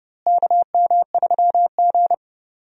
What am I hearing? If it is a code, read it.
Morse code, heard as a keyed tone, about 30 wpm, 700 Hz. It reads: KM3G
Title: USA Call Signs - 4 Characters - Repeated 2x - 30wpm